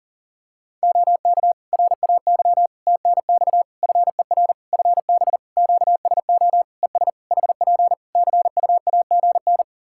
Morse code, heard as a keyed tone, about 40 wpm, 700 Hz. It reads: OK RAY TNX FER FB QSO ES HP CUAGN
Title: Practice QSO - Exchange 4 - Ending - 40wpm